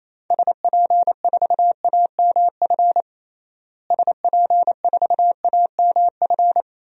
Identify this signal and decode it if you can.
Morse code, heard as a keyed tone, about 28 wpm, 700 Hz. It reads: SP4AMF SP4AMF
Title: DXCC-100 Call Signs - 6 Characters - Repeated 2x - 28wpm